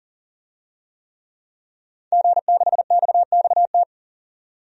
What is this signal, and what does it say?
Morse code, heard as a keyed tone, about 40 wpm, 700 Hz. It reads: G6XXT
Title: DXCC-100 Call Signs - 5 Characters - 40wpm